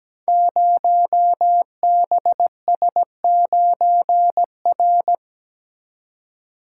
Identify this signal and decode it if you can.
Morse code, heard as a keyed tone, about 17 wpm, 700 Hz. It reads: 0BS9R
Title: Code Groups - 5 Letter:Numbers - 17wpm